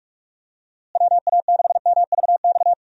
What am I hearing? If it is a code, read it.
Morse code, heard as a keyed tone, about 46 wpm, 700 Hz. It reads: WA6MVX
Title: USA Call Signs - 6 Characters - Repeated 2x - 45wpm